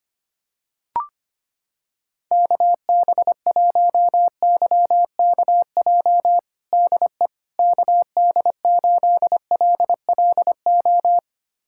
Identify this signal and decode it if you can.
Morse code, heard as a keyed tone, about 25 wpm, 700 Hz. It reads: KB1YKJ DE KD8LLO
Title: USA Callsign DE Callsign - 6 Characters - 25wpm